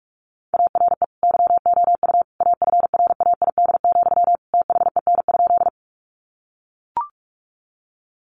Morse code, heard as a keyed tone, about 44 wpm, 700 Hz. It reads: ARE YOU AFRAID, THEN?
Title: Sentences from Top 1500 Words 45wpm